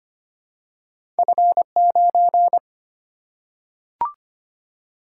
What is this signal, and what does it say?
Morse code, heard as a keyed tone, about 25 wpm, 700 Hz. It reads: F9
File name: Two Letter-Number Combinations - 25wpm